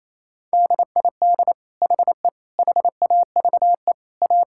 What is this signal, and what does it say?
Morse code, heard as a keyed tone, about 27 wpm, 700 Hz. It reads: DID HE HAVE A
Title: Sentences from Top 1000 Words - Review Part 6 - 28wpm